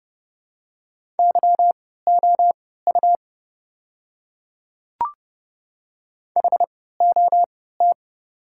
Morse code, heard as a keyed tone, about 30 wpm, spaced as for 17 wpm, 700 Hz. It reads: YOU HOT
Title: 3x Character Spacing - Top 100 Words - 30wpm